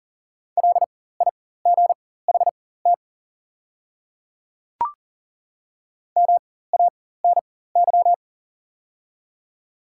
Morse code, heard as a keyed tone, about 40 wpm, spaced as for 19 wpm, 700 Hz. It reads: LIGHT MANY
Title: 4x Character Spacing - Top 200 Words - 40wpm